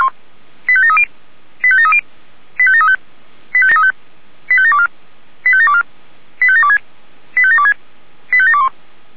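Five-tone selective call (ZVEI), heard as quick runs of five tones, tones of about 80 ms. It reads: 85730 85730 85735 85735 85724 85724 85726 85726 85721
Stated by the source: ZVEI five-tone selective call received by SDR.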